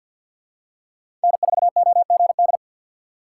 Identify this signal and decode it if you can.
Morse code, heard as a keyed tone, about 47 wpm, 700 Hz. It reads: N4OGD
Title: USA Call Signs - 5 Characters - 3x Character Spacing - 50wpm